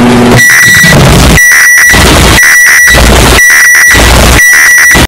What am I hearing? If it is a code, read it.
There is no code.